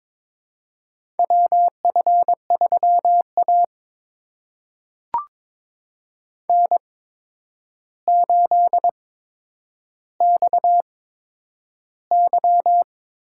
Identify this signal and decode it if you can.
Morse code, heard as a keyed tone, about 22 wpm, spaced as for 9 wpm, 700 Hz. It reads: WF3A N8XY
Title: USA Call Signs - 4 Characters - 8x Character Spacing - 22wpm